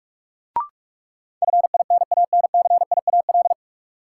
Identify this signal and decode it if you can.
Morse code, heard as a keyed tone, about 47 wpm, 700 Hz. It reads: FINANCIAL
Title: Sentences from Top 1000 Words - Review Part 5 - New Word Drill - 45wpm